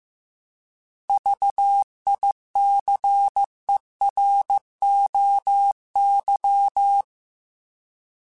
Morse code, heard as a keyed tone, about 15 wpm, 800 Hz. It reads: VICEROY
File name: Alphabetical / Alphabetical mp3s / Vs @13